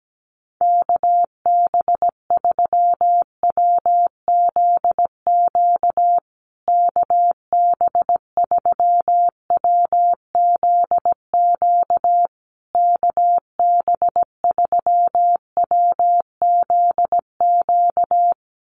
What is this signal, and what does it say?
Morse code, heard as a keyed tone, about 17 wpm, 700 Hz. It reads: KB3WZQ KB3WZQ KB3WZQ
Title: USA Call Signs Repeated 3x 17wpm